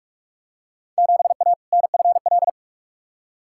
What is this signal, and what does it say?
Morse code, heard as a keyed tone, about 46 wpm, 700 Hz. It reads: 7A NFL
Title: Field Day Exchanges Encore - 45wpm